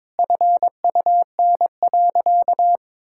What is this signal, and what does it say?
Morse code, heard as a keyed tone, about 22 wpm, 700 Hz. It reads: FUN.
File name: practice / 22wpm / Sentences with 2-Word Phrases - 22wpm